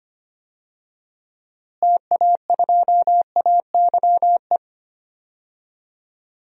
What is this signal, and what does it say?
Morse code, heard as a keyed tone, about 25 wpm, 700 Hz. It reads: TA2AYE
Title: DXCC-100 Call Signs - 6 Characters - Repeated 2x - 25wpm